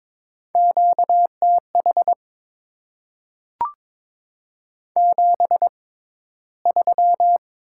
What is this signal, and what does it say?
Morse code, heard as a keyed tone, about 22 wpm, spaced as for 13 wpm, 690 Hz. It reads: QTH 73
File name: Top 20 QSO Elements - 6x Character Spacing - 22wpm